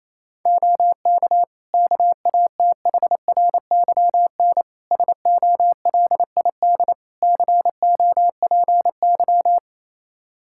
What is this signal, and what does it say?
Morse code, heard as a keyed tone, about 28 wpm, 700 Hz. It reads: OK KATHRYN SOLID COPY
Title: K1USN SST QSO Practice 28wpm